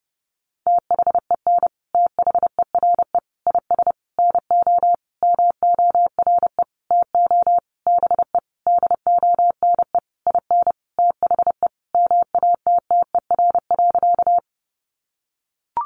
Morse code, heard as a keyed tone, about 30 wpm, 700 Hz. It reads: THEN THERE IS NO MORE TO BE DONE IN THE MATTER.
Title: Sentences from Top 700 Words 30wpm